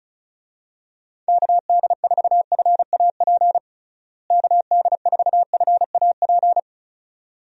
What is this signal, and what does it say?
Morse code, heard as a keyed tone, about 36 wpm, 700 Hz. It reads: KD4FAP KD4FAP
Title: USA Call Signs - 6 Characters - Repeated 2x - 35wpm